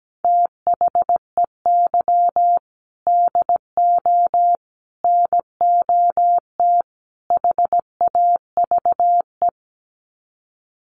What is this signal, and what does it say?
Morse code, heard as a keyed tone, about 17 wpm, 700 Hz. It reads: THEY DO NOT HAVE
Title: Sets of 4 Words 17wpm